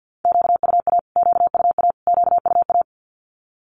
Morse code, heard as a keyed tone, about 50 wpm, 700 Hz. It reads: QUA QUA QUA
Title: Q-Codes Repeated 3x 50wpm